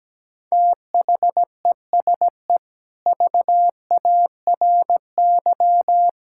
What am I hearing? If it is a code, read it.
Morse code, heard as a keyed tone, about 17 wpm, 700 Hz. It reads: THESE VARY
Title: Sentences from Top 1600 Words 17wpm